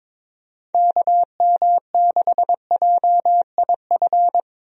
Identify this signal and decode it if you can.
Morse code, heard as a keyed tone, about 22 wpm, 700 Hz. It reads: KM6JIF
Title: USA Call Signs - 6 Characters - Repeated 2x - 22wpm